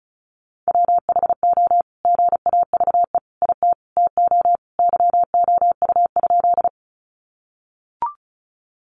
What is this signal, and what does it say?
Morse code, heard as a keyed tone, about 34 wpm, 700 Hz. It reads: WHO GAVE IT TO YOU?
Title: Sentences from Top 500 Words 35wpm